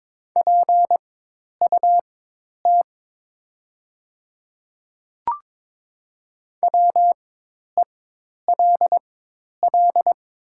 Morse code, heard as a keyed tone, about 22 wpm, spaced as for 10 wpm, 700 Hz. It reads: PUT WELL
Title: 4x Character Spacing - Top 200 Words - 22wpm